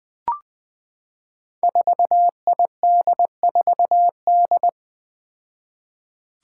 Morse code, heard as a keyed tone, about 21 wpm, 700 Hz. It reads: 4ID4D
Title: Five Challenging Letter-Number Combinations 20wpm